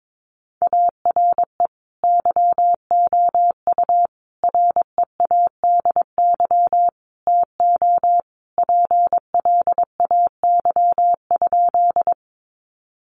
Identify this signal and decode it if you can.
Morse code, heard as a keyed tone, about 22 wpm, 700 Hz. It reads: ARE YOU READY TO PLAY?
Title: Sentences from Top 400 Words 22wpm